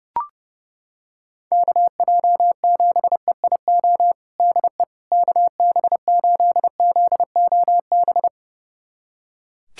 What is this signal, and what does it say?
Morse code, heard as a keyed tone, about 30 wpm, 700 Hz. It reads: KJ7EIO DE KB8ZOB
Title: USA Callsign DE Callsign 30wpm